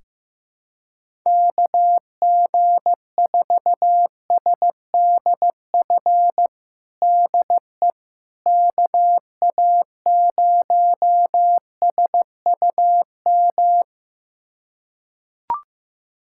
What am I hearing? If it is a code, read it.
Morse code, heard as a keyed tone, about 16 wpm, 700 Hz. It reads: KG4SDF DE KA0SUM
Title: USA Callsign DE Callsign - 6 Characters - 15wpm